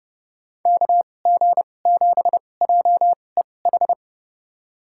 Morse code, heard as a keyed tone, about 32 wpm, spaced as for 22 wpm, 700 Hz. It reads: KG7JEH
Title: USA Call Signs - 6 Characters - 2x Character Spacing - 30wpm